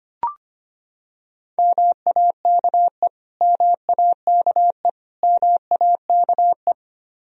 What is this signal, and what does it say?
Morse code, heard as a keyed tone, about 25 wpm, 700 Hz. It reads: MAKE MAKE MAKE
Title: Top 100 Words Repeated 25wpm